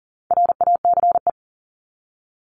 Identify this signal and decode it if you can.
Morse code, heard as a keyed tone, about 40 wpm, 700 Hz. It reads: RACE
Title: Top 1501-1750 Words 40wpm